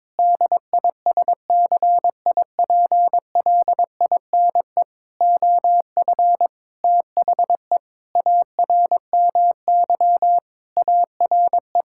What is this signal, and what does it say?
Morse code, heard as a keyed tone, about 22 wpm, 700 Hz. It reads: DISCIPLINE OF THE ARMY ARE
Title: Sentences from Top 1000 Words - Review Part 6 - 22wpm